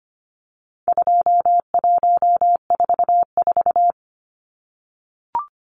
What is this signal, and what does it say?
Morse code, heard as a keyed tone, about 25 wpm, 700 Hz. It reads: 2144
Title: Numbers - 4-Digit 25wpm